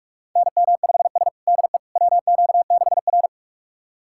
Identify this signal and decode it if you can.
Morse code, heard as a keyed tone, about 46 wpm, 700 Hz. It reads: NM5S DE WQ6R
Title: USA Callsign DE Callsign - 4 Characters - 45wpm